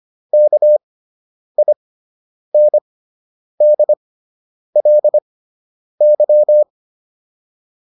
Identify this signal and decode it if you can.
Morse code, heard as a keyed tone, about 25 wpm, spaced as for 9 wpm, 600 Hz. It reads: KINDLY